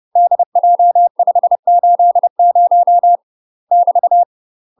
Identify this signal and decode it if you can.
Morse code, heard as a keyed tone, about 30 wpm, 700 Hz. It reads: DJ580 <BT>
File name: UK MASTER QSOs TRAINING / 30WPM / UK MASTER QSOs 0034